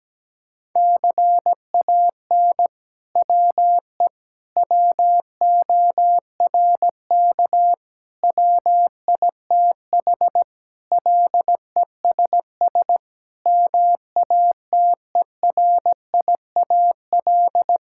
Morse code, heard as a keyed tone, about 17 wpm, 700 Hz. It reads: CAN WE WORK WITH LESS MATERIAL
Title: Sentences from Top 1000 Words - Review Part 5 - 17wpm